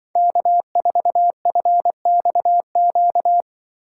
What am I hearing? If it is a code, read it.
Morse code, heard as a keyed tone, about 24 wpm, 700 Hz. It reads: K4FXQ
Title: Speed Racing - USA 1x3 Call Signs 20wpm